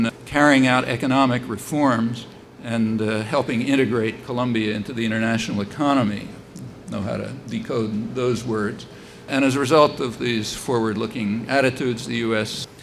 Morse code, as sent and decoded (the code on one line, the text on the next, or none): none